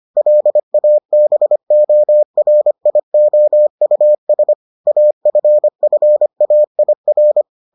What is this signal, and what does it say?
Morse code, heard as a keyed tone, about 25 wpm, 600 Hz. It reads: LABORIOUS AFFAIR